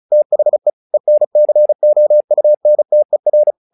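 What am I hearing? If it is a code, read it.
Morse code, heard as a keyed tone, about 35 wpm, 600 Hz. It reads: THE ENCOUNTER
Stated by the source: Morse practice text "Tower Trouble" sent at 35 WPM, from the SKCC - Straight Key Century Club learning center audio files.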